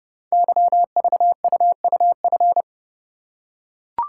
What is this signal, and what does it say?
Morse code, heard as a keyed tone, about 30 wpm, 700 Hz. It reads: YVUUF
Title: Code Groups - 5 Letter:Numbers - 30wpm